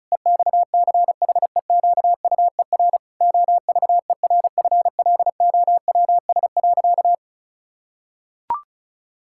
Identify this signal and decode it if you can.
Morse code, heard as a keyed tone, about 36 wpm, 700 Hz. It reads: EXCHEQUER OVERFLOWS.
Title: Pangram Challenge v2 - 35wpm